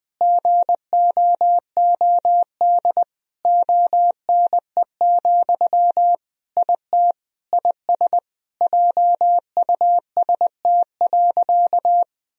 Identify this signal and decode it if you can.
Morse code, heard as a keyed tone, about 20 wpm, 700 Hz. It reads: GOOD ONE, IT IS JUST.
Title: Sentences from Top 1000 Words - Review Part 1 - 20wpm